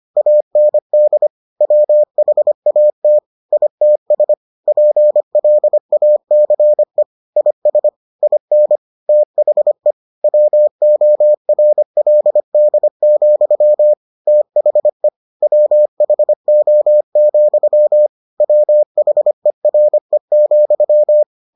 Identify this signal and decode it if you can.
Morse code, heard as a keyed tone, about 25 wpm, 600 Hz. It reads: AND WHAT ITS PLACE IS IN THE WORLD, THE WHO, WHERE,